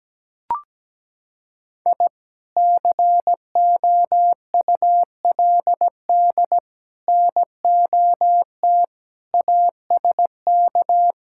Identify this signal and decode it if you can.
Morse code, heard as a keyed tone, about 17 wpm, 700 Hz. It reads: I COULD NOT ASK
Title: Sentences from Top 1000 Words - Review Part 5 - 17wpm